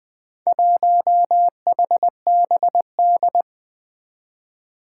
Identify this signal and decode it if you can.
Morse code, heard as a keyed tone, about 20 wpm, 700 Hz. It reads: 1HBD
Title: Four Challenging Letter-Number Combinations 20wpm